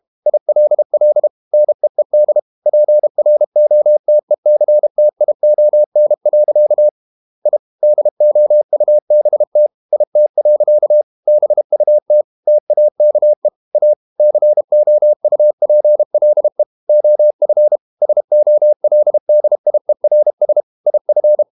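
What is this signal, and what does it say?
Morse code, heard as a keyed tone, about 32 wpm, 600 Hz. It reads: ILL NEED PROTECTION. I DOUBT IT. BUT TAKE A COUPLE OF SOLDIERS IF
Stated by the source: Morse code practice at 32 wpm from CW med Gnister, reading Ken Follett's novel Never in English.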